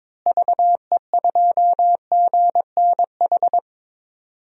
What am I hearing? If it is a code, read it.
Morse code, heard as a keyed tone, about 22 wpm, 700 Hz. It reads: VE2GNH
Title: DXCC-100 Call Signs - 6 Characters - Repeated 3x - 22wpm